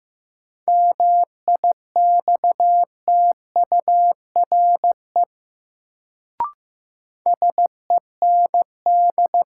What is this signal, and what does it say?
Morse code, heard as a keyed tone, about 15 wpm, 700 Hz. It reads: MIXTURE SEND
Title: Sentences from Top 1500 Words - New Word Drill Alt - 15wpm